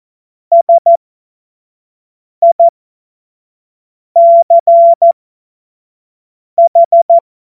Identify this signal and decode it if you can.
Morse code, heard as a keyed tone, about 14 wpm, spaced as for 5 wpm, 680 Hz. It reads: SICH
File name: Morsekurs / Witze / Witz 09 April 2024 25 F70